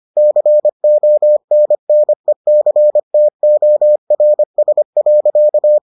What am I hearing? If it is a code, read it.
Morse code, heard as a keyed tone, about 25 wpm, 600 Hz. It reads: CONNECTORS.